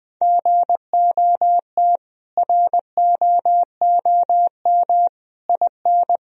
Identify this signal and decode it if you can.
Morse code, heard as a keyed tone, about 21 wpm, 700 Hz. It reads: GOT ROOM IN
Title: Sentences from Top 1000 Words - Review Part 1 - 20wpm